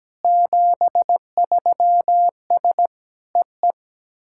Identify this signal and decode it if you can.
Morse code, heard as a keyed tone, about 17 wpm, 700 Hz. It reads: 73S EE
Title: SOTA QSO Practice 17wpm